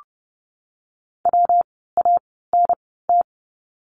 Morse code, heard as a keyed tone, about 30 wpm, spaced as for 17 wpm, 700 Hz. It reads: WANT
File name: course / 30:10wpm / 237 Lesson 38f - Letter q - Words 30wpm